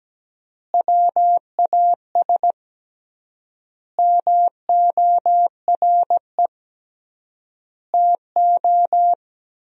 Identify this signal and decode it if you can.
Morse code, heard as a keyed tone, about 17 wpm, 700 Hz. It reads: WAS MORE TO
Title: 3x Word-Spacing - Sentences from Top 100 Words - 17wpm